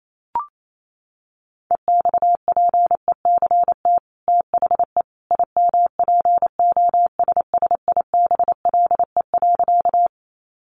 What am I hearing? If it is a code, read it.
Morse code, heard as a keyed tone, about 28 wpm, 700 Hz. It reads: EXPECT THE IMPOSSIBLE.
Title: Sentences from Top 1100 Words 28wpm